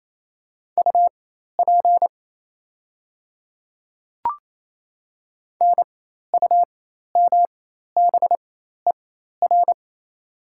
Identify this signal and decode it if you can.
Morse code, heard as a keyed tone, about 28 wpm, spaced as for 13 wpm, 700 Hz. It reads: UP NUMBER
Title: 4x Character Spacing - Top 100 Words - 28wpm